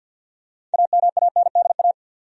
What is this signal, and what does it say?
Morse code, heard as a keyed tone, about 47 wpm, 700 Hz. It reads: AMANDA
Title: USA Common Names - 50wpm